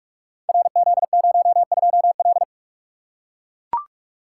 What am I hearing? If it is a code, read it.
Morse code, heard as a keyed tone, about 44 wpm, 700 Hz. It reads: RZ02L
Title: Code Groups - 5 Letter:Numbers - 45wpm